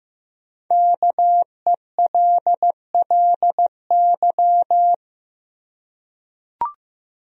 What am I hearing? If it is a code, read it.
Morse code, heard as a keyed tone, about 15 wpm, 700 Hz. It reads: KELLY
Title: USA Common Names - 15wpm